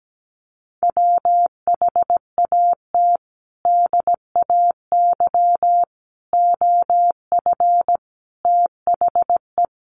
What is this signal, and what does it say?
Morse code, heard as a keyed tone, about 17 wpm, 700 Hz. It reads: WHAT DAY OF THE
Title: Sentences from Top 800 Words 17wpm